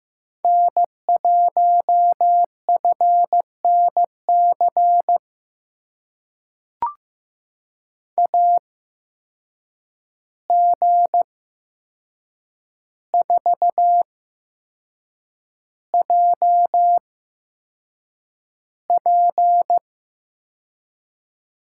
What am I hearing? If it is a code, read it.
Morse code, heard as a keyed tone, about 15 wpm, spaced as for 6 wpm, 700 Hz. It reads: N1FNC AG4JP
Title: USA Call Signs - 5 Characters - 8x Character Spacing - 15wpm